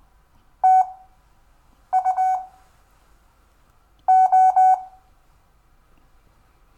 Morse code, heard as a keyed tone, about 20 wpm, spaced as for 7 wpm, 750 Hz. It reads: TU O
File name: cw / Lesson5 LO